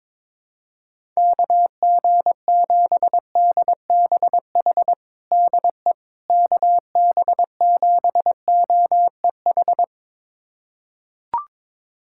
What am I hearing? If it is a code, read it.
Morse code, heard as a keyed tone, about 22 wpm, 700 Hz. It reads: KG7DBH DE KB7OEH
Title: USA Callsign DE Callsign - 6 Characters - 22wpm